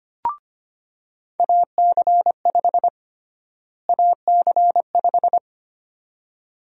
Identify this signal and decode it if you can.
Morse code, heard as a keyed tone, about 25 wpm, 700 Hz. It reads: AC5 AC5
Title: USA Call Sign Prefixes - Repeated 2x - 25wpm